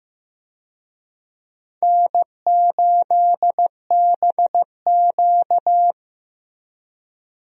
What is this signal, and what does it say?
Morse code, heard as a keyed tone, about 15 wpm, 700 Hz. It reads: N8BQ
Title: USA Call Signs - 4 Characters - Repeated 3x - 15wpm